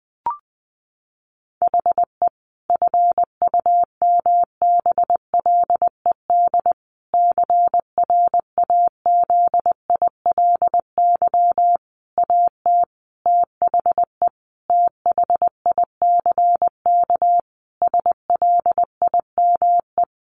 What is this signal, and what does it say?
Morse code, heard as a keyed tone, about 20 wpm, 700 Hz. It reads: HE FUMBLED CRAZILY AT THE THICK SLIME
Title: A Sound Of Thunder 20wpm